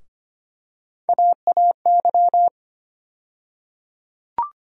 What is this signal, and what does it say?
Morse code, heard as a keyed tone, about 25 wpm, 700 Hz. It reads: AAY
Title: USA Call Sign Postfixes - 3 Letters 25wpm